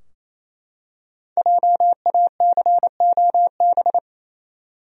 Morse code, heard as a keyed tone, about 28 wpm, 700 Hz. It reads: JACOB
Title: USA Common Names - Max 5 Letters - 28wpm